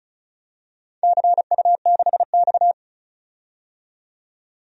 Morse code, heard as a keyed tone, about 36 wpm, 700 Hz. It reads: CU6X
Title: DXCC-100 Call Signs - 4 Characters - 35wpm